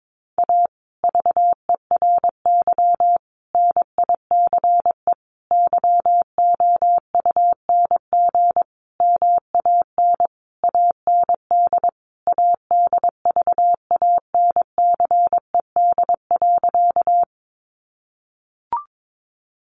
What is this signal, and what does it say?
Morse code, heard as a keyed tone, about 22 wpm, 700 Hz. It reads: A VERY NICE YOUNG MAN AND ADVANCED.
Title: Sentences from Top 1200 Words 22wpm